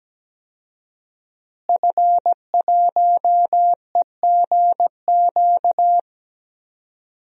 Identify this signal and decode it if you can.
Morse code, heard as a keyed tone, about 17 wpm, 700 Hz. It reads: F1EGQ
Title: DXCC-100 Call Signs - 5 Characters - Repeated 2x - 17wpm